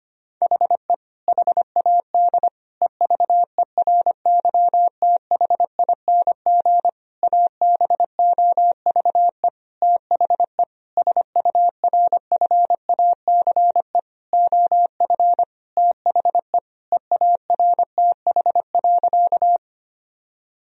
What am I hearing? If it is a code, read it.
Morse code, heard as a keyed tone, about 25 wpm, 700 Hz. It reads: HE HAD EVERYTHING ABOVE THE SURFACE OF THE EARTH.
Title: Sentences from Top 1000 Words - Review Part 3 - 25wpm